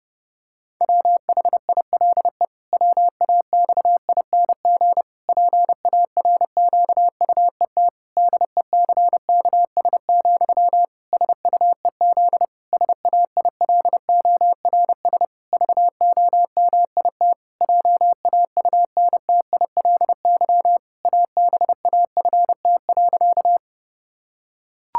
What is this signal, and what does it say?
Morse code, heard as a keyed tone, about 30 wpm, 700 Hz. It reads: WHILE WAXING PARQUET DECKS, SUEZ SAILORS VOMIT JAUNTILY ABAFT.
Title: Pangram Challenge v2 - 30wpm